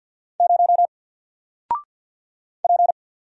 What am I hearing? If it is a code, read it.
Morse code, heard as a keyed tone, about 48 wpm, 700 Hz. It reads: 0 P
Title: Single Letter-Number 50wpm